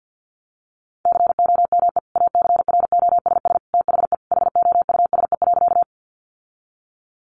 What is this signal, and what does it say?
Morse code, heard as a keyed tone, about 50 wpm, 700 Hz. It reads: COME ACROSS THE HOUSE.
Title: Sentences from Top 700 Words 50wpm